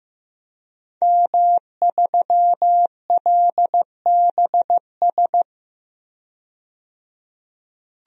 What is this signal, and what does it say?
Morse code, heard as a keyed tone, about 15 wpm, 700 Hz. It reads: M3LBS
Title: DXCC-100 Call Signs - 5 Characters - 15wpm